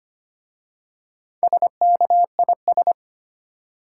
Morse code, heard as a keyed tone, about 25 wpm, 690 Hz. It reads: SKIS
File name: practice / 25wpm / Easy Words with Hints - 25wpm